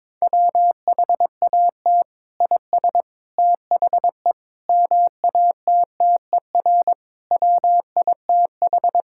Morse code, heard as a keyed tone, about 22 wpm, 700 Hz. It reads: WHAT IS THE MATTER WITH
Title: Sentences from Top 1000 Words - Review Part 5 - 22wpm